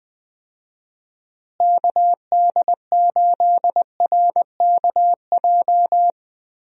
Morse code, heard as a keyed tone, about 20 wpm, 700 Hz. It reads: KD8RKJ
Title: USA Call Signs - 6 Characters - Repeated 3x - 20wpm